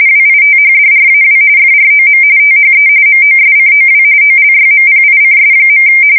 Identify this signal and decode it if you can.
RTTY, about 45.45 baud. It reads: WITHDRAW FROM THE MOSCOW OLYMPICS.
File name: fsk170 / USS-New-Jersey